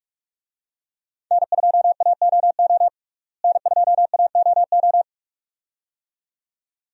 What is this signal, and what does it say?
Morse code, heard as a keyed tone, about 46 wpm, 700 Hz. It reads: N2AOO N2AOO
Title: USA Call Signs - 5 Characters - Repeated 2x - 45wpm